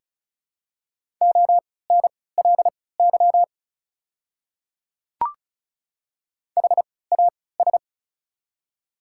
Morse code, heard as a keyed tone, about 36 wpm, spaced as for 20 wpm, 700 Hz. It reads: ONLY HAS
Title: 3x Character Spacing - Top 200 Words - 35wpm